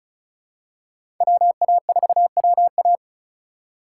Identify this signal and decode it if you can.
Morse code, heard as a keyed tone, about 36 wpm, 700 Hz. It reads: WA4WA
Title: USA Call Signs - 8x Character Spacing - 35wpm